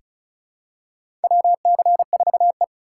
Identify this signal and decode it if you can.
Morse code, heard as a keyed tone, about 36 wpm, 700 Hz. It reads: WC4E
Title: USA Call Signs - 4 Characters - 8x Character Spacing - 35wpm